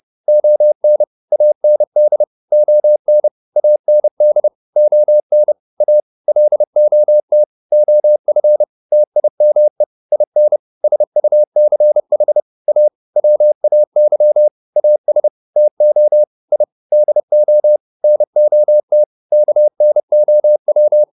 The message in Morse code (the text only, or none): ON AND ON AND ON A LOT OF TIME IN SUCH A WAY AS TO I DO NOT KNOW